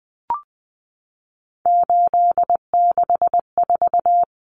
Morse code, heard as a keyed tone, about 20 wpm, 700 Hz. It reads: 864
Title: Common Cut Numbers - 3-Digit 20wpm